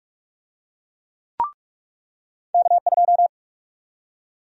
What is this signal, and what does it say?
Morse code, heard as a keyed tone, about 44 wpm, 700 Hz. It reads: K2
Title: USA Call Sign Prefixes - 45wpm